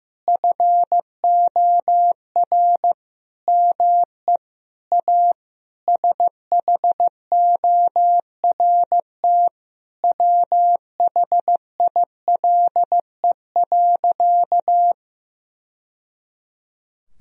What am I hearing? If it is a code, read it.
Morse code, heard as a keyed tone, about 15 wpm, 700 Hz. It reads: FOR ME A SHORT WHILE.
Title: Sentences from Top 1000 Words - Review Part 2 - 15wpm